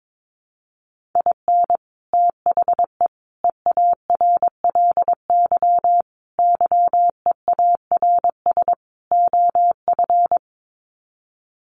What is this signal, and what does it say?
Morse code, heard as a keyed tone, about 22 wpm, 700 Hz. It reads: IN THE EARLY YEARS OF
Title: Sets of 5 Words Encore 22wpm